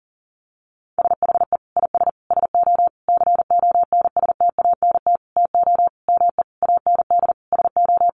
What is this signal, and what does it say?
Morse code, heard as a keyed tone, about 40 wpm, 700 Hz. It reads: SHE IS SO CONSTANT TO ME AND SO
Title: Sentences from Top 1500 Words 40wpm